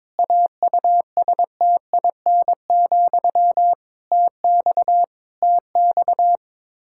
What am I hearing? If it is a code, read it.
Morse code, heard as a keyed tone, about 22 wpm, 700 Hz. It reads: AUSTIN, TX TX
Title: K1USN SST QSO Practice 22wpm